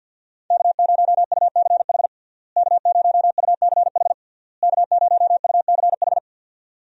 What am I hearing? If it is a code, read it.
Morse code, heard as a keyed tone, about 52 wpm, 700 Hz. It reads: K0UCH K0UCH K0UCH
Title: USA Call Signs - 5 Characters - Repeated 3x - 50wpm